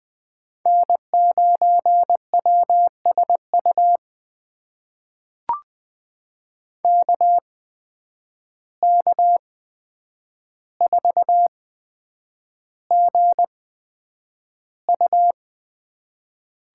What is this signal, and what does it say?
Morse code, heard as a keyed tone, about 20 wpm, spaced as for 9 wpm, 700 Hz. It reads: N9WSU KK4GU
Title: USA Call Signs - 8x Character Spacing - 20wpm